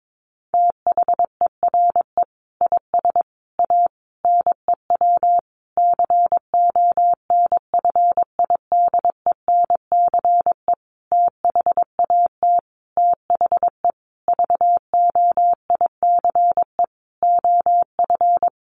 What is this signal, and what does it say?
Morse code, heard as a keyed tone, about 22 wpm, 700 Hz. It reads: THERE IS A NEW CONFIDENCE THAT THE VOICE OF
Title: Sentences from Top 1500 Words 22wpm